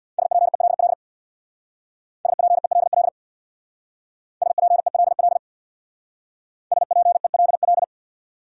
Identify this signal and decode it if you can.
Morse code, heard as a keyed tone, about 64 wpm, 700 Hz. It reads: SPELL SPELL SPELL SPELL